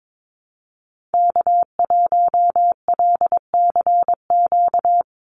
Morse code, heard as a keyed tone, about 22 wpm, 700 Hz. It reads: K1LCQ
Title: USA Call Signs 22wpm